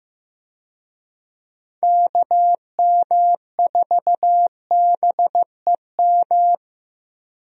Morse code, heard as a keyed tone, about 16 wpm, 700 Hz. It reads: KM4BEM
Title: USA Call Signs - 6 Characters - Repeated 2x - 15wpm